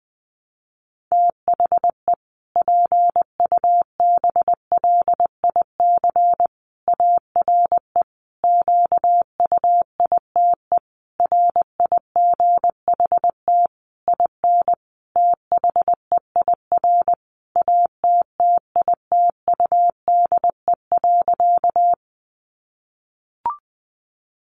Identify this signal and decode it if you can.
Morse code, heard as a keyed tone, about 20 wpm, 700 Hz. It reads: THE PUBLIC ARE QUITE RIGHT IN THEIR ATTITUDE.
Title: Sentences from Top 1300 Words 20wpm